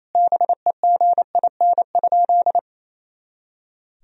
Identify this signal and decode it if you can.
Morse code, heard as a keyed tone, about 28 wpm, 700 Hz. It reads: BEGIN?
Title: Sentences from Top 300 Words and Abbreviations 28wpm